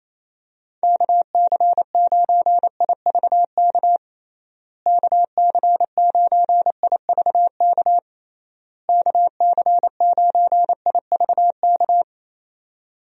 Morse code, heard as a keyed tone, about 28 wpm, 700 Hz. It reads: KC9IVK KC9IVK KC9IVK
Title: USA Call Signs - 6 Characters - Repeated 3x - 28wpm